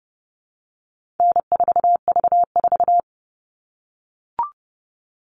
Morse code, heard as a keyed tone, about 30 wpm, 700 Hz. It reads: N4V4
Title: All Cut Numbers - 4-Digit 30wpm